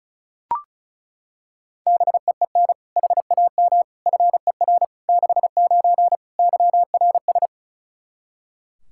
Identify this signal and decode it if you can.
Morse code, heard as a keyed tone, about 34 wpm, 700 Hz. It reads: BEEN HAM FER 69 YRS
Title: Practice QSO - Exchange 3 - More Info - 35wpm